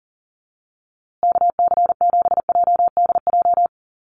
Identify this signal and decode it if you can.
Morse code, heard as a keyed tone, about 37 wpm, 700 Hz. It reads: KC7JDJ
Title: USA Call Signs Repeated 2x 40wpm